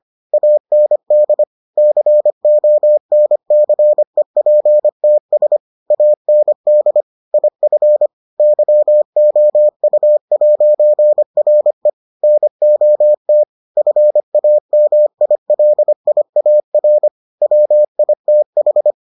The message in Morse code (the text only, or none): AND CONCEPTS AND IF YOU'RE NOT FAMILIAR WITH